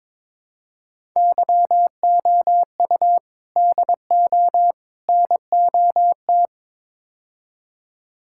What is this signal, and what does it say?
Morse code, heard as a keyed tone, about 22 wpm, 700 Hz. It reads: YOU DO NOT
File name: practice / 22wpm / Build A Phrase - 5-Word Phrases - 22wpm